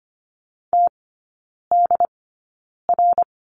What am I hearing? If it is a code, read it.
Morse code, heard as a keyed tone, about 25 wpm, spaced as for 9 wpm, 700 Hz. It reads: TDR
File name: Lesson 17e - Letter u - Three Characters 25wpm